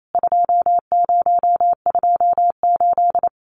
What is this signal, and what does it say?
Morse code, heard as a keyed tone, about 28 wpm, 700 Hz. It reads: 2028
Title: CWT Exchange 28wpm